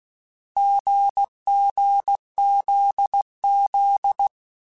Morse code, heard as a keyed tone, about 16 wpm, 800 Hz. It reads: GGZZ
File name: Difficult Combos / Difficult Combos-mp3s / GZ GZ7 @15